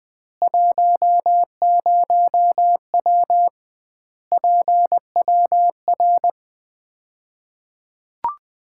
Morse code, heard as a keyed tone, about 20 wpm, 700 Hz. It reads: 10W PWR
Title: Common QSO Phrases - 2x Word-Spacing - 20wpm